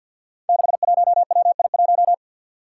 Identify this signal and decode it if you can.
Morse code, heard as a keyed tone, about 51 wpm, 700 Hz. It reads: 61WI1
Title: Five Challenging Letter-Number Combinations 50wpm